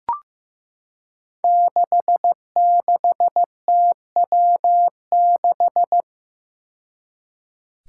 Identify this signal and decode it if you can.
Morse code, heard as a keyed tone, about 15 wpm, 700 Hz. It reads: 66TW6
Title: Five Challenging Letter-Number Combinations 15wpm